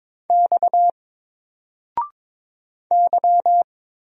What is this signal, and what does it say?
Morse code, heard as a keyed tone, about 22 wpm, 700 Hz. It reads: X Y
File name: Single-Letter-Number-22wpm